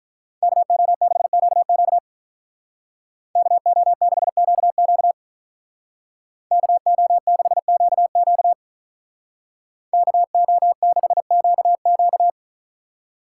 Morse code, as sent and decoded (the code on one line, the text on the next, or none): -.- --- -.... --.- --.- / -.- --- -.... --.- --.- / -.- --- -.... --.- --.- / -.- --- -.... --.- --.-
KO6QQ KO6QQ KO6QQ KO6QQ